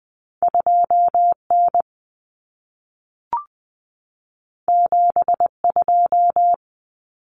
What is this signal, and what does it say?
Morse code, heard as a keyed tone, about 20 wpm, 700 Hz. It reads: 2N 72
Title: Common Cut Numbers - 2-Digit 20wpm